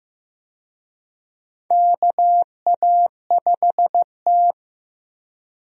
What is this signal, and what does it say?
Morse code, heard as a keyed tone, about 15 wpm, 700 Hz. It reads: KA5T
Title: USA Call Signs - 4 Characters - 15wpm